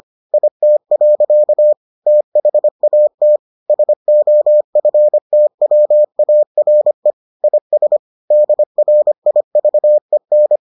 Morse code, heard as a keyed tone, about 25 wpm, 600 Hz. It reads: IT. THAT SOFTWARE IS DRIVEN